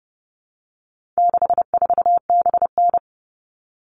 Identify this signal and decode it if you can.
Morse code, heard as a keyed tone, about 30 wpm, 700 Hz. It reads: 64BN
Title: All Cut Numbers - 4-Digit 30wpm